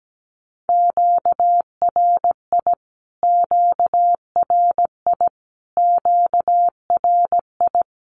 Morse code, heard as a keyed tone, about 17 wpm, 700 Hz. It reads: QRI QRI QRI
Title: Q-Codes Repeated 3x 17wpm